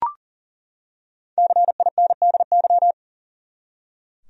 Morse code, heard as a keyed tone, about 39 wpm, 700 Hz. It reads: CINDY
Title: USA Common Names - Max 5 Letters - 40wpm